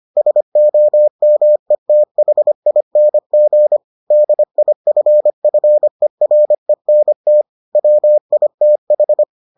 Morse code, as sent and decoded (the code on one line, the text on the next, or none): ... --- -- . - .... .. -. --. / -.. .. ..-. ..-. .-. -. - / .-- .. - ....
SOMETHING DIFFERENT WITH